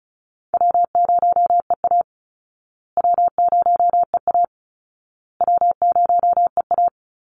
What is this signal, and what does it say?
Morse code, heard as a keyed tone, about 34 wpm, 700 Hz. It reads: W0EA W0EA W0EA
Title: Easy Call Signs Repeated 3x - 35wpm